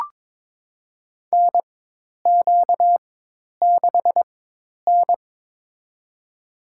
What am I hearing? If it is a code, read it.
Morse code, heard as a keyed tone, about 23 wpm, spaced as for 10 wpm, 700 Hz. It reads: NQ6N
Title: USA Call Signs - 4 Characters - 4x Character Spacing - 22wpm